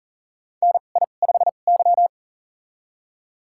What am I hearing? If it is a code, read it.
Morse code, heard as a keyed tone, about 40 wpm, spaced as for 29 wpm, 700 Hz. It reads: NI5Y
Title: USA Call Signs - 2x Character Spacing - 40wpm